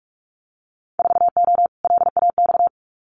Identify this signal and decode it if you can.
Morse code, heard as a keyed tone, about 46 wpm, 700 Hz. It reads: 4O LAX
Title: Winter Field Day Exchanges Encore 45wpm